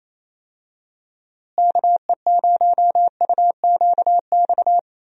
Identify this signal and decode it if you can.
Morse code, heard as a keyed tone, about 27 wpm, 700 Hz. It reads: KE0UQX
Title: USA Call Signs - 6 Characters - Repeated 3x - 28wpm